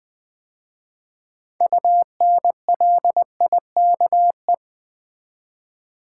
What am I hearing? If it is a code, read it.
Morse code, heard as a keyed tone, about 20 wpm, 700 Hz. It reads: UNLIKE
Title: 6 Letter Words - First 3 Letters Given - 20wpm